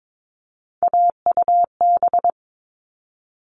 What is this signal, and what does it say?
Morse code, heard as a keyed tone, about 22 wpm, 700 Hz. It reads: AUB
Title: All Cut Numbers - 3-Digit 22wpm